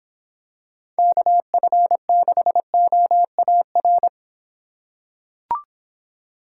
Morse code, heard as a keyed tone, about 24 wpm, 700 Hz. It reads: KF6OAR